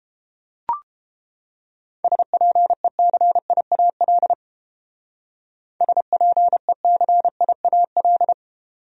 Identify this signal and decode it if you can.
Morse code, heard as a keyed tone, about 30 wpm, 700 Hz. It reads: SPECIAL SPECIAL